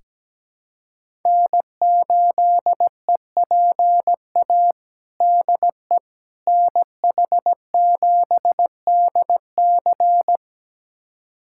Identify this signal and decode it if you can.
Morse code, heard as a keyed tone, about 17 wpm, 700 Hz. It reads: N8EPA DE NH7DC
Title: USA Callsign DE Callsign - 5 Characters - 17wpm